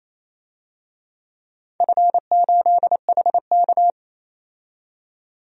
Morse code, heard as a keyed tone, about 29 wpm, 700 Hz. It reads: F8HK
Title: DXCC-100 Call Signs - 4 Characters - 28wpm